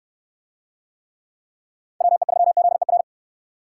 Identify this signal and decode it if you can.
Morse code, heard as a keyed tone, about 61 wpm, 700 Hz. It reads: WE3ZER